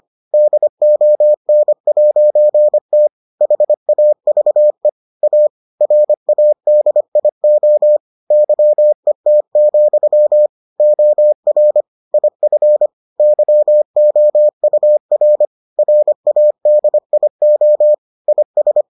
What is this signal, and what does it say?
Morse code, heard as a keyed tone, about 25 wpm, 600 Hz. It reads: DON'T HAVE A RADIO YET, OR IF YOUR RADIO IS